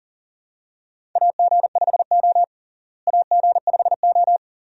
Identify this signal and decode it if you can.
Morse code, heard as a keyed tone, about 40 wpm, 700 Hz. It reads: AG5O AG5O
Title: USA Call Signs - 4 Characters - Repeated 2x - 40wpm